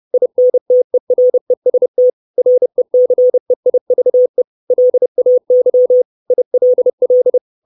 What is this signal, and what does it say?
Morse code, heard as a keyed tone, about 30 wpm, 500 Hz. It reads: INTEREST RECEIVE LAY ILL
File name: 632 Words 30WPM